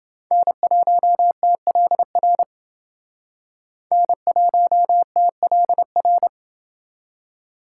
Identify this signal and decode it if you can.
Morse code, heard as a keyed tone, about 29 wpm, 700 Hz. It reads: N1TLR N1TLR